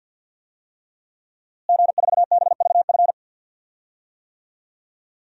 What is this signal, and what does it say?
Morse code, heard as a keyed tone, about 52 wpm, 700 Hz. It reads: G4BVF
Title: DXCC-100 Call Signs - 5 Characters - Repeated 3x - 50wpm